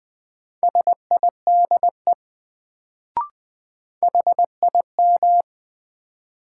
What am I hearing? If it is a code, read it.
Morse code, heard as a keyed tone, about 20 wpm, 700 Hz. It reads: SIDE HIM
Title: Sets of 2 Words - New Word Drill - 20wpm